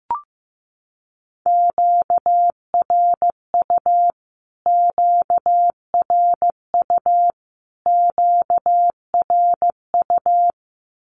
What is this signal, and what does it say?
Morse code, heard as a keyed tone, about 15 wpm, 700 Hz. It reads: QRU QRU QRU
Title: Q-Codes Repeated 3x 15wpm